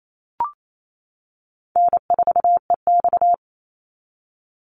Morse code, heard as a keyed tone, about 28 wpm, 700 Hz. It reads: N4EX
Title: NA SOTA Chaser Callsigns 28wpm